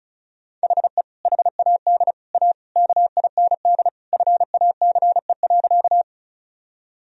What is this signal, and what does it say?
Morse code, heard as a keyed tone, about 36 wpm, 700 Hz. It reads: HE HAD A KIND FACE.
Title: Sentences from Top 1000 Words - Review Part 2 - 35wpm